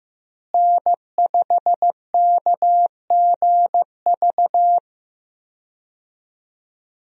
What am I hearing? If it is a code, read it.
Morse code, heard as a keyed tone, about 15 wpm, 700 Hz. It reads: N5KGV
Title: USA Call Signs - 5 Characters - Repeated 2x - 15wpm